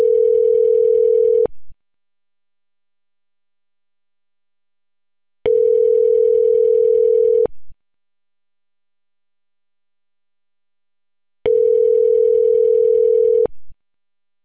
Ringback tone (North America), 440 + 480 Hz.